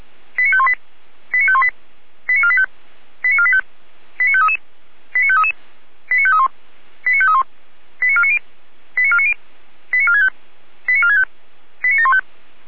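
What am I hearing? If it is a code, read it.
Five-tone selective call (ZVEI), heard as quick runs of five tones, tones of about 70 ms. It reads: 89518 89518 89485 89485 89533 89533 89531 89531 89490 89490 89465 89465 89715
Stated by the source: Five-tone ZVEI selective call received on an SDR.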